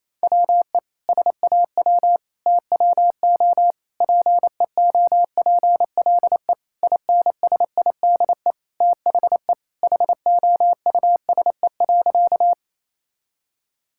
Morse code, heard as a keyed tone, about 28 wpm, 700 Hz. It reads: WE SAW TWO PEOPLE INSIDE THE HOUSE.
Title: Sentences from Top 1000 Words - Review Part 6 - 28wpm